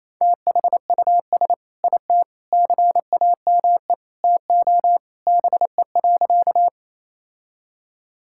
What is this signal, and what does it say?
Morse code, heard as a keyed tone, about 28 wpm, 700 Hz. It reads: THUS IT CAME TO BE.
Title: Sentences from Top 1000 Words - Review Part 5 - 28wpm